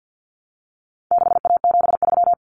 Morse code, heard as a keyed tone, about 48 wpm, 700 Hz. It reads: 6A73
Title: Common Cut Numbers - 4-Digit 50wpm